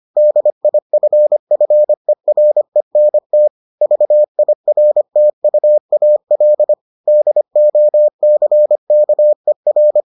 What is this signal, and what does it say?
Morse code, heard as a keyed tone, about 25 wpm, 600 Hz. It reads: DIFFERENT VIRTUAL DOCKER